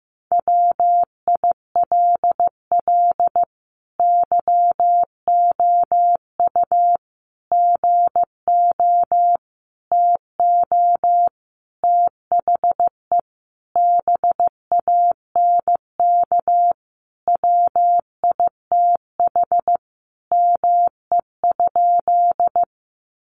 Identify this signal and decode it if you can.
Morse code, heard as a keyed tone, about 15 wpm, 700 Hz. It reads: WILL YOU GO TO THE BANK WITH ME?
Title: Sentences from Top 900 Words 15wpm